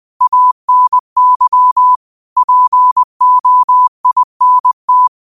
Morse code, heard as a keyed tone, about 20 wpm, 1000 Hz. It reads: ANY POINT
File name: qotdfiles / 20wpm